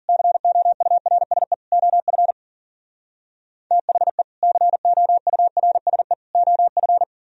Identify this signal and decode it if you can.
Morse code, heard as a keyed tone, about 52 wpm, 700 Hz. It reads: COURSE OF THE COURSE OF